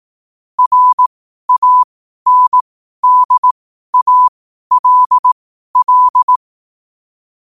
Morse code, heard as a keyed tone, about 18 wpm, spaced as for 12 wpm, 1000 Hz. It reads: RANDALL
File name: qotdfiles / 12wpm